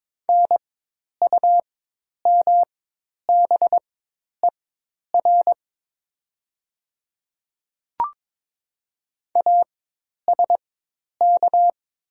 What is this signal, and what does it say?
Morse code, heard as a keyed tone, about 22 wpm, spaced as for 10 wpm, 700 Hz. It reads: NUMBER ASK
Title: 4x Character Spacing - Top 300 Words - 22wpm